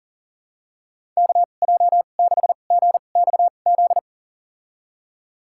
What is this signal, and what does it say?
Morse code, heard as a keyed tone, about 40 wpm, spaced as for 29 wpm, 700 Hz. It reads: KJ6GXZ
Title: USA Call Signs - 6 Characters - 2x Character Spacing - 40wpm